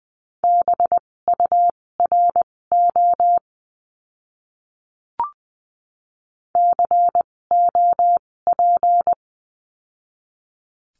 Morse code, heard as a keyed tone, about 20 wpm, spaced as for 16 wpm, 700 Hz. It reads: BURO COP
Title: 256 Lesson 42a - Words 12-20wpm